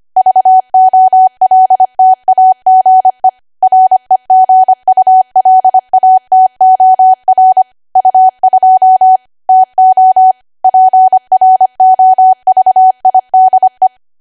Morse code, heard as a keyed tone, about 25 wpm, 750 Hz. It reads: VOLTAGE REGULATOR U2 TO PROVIDE
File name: Archive / 25 WPM / 25WPM